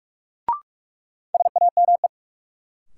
Morse code, heard as a keyed tone, about 45 wpm, 700 Hz. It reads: SAME
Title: Sets of 2 Words - New Word Drill - 45wpm